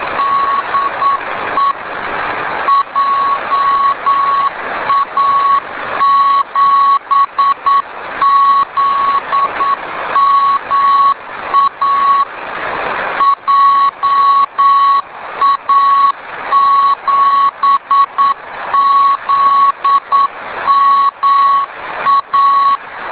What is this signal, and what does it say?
Morse code, heard as a keyed tone, about 9 wpm, 1110 Hz, with noise. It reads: DE JA7ZMA JA7ZMA